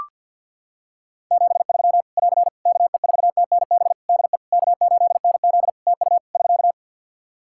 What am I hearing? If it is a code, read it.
Morse code, heard as a keyed tone, about 45 wpm, 700 Hz. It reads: 73 <AR> KE4TNB DE K8NZ TU <SK>
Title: Practice QSO - Exchange 4 - Ending - 50wpm